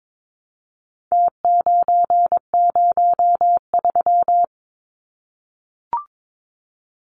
Morse code, heard as a keyed tone, about 22 wpm, 700 Hz. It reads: T903
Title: Numbers - Contest Style - 4-Digit 22wpm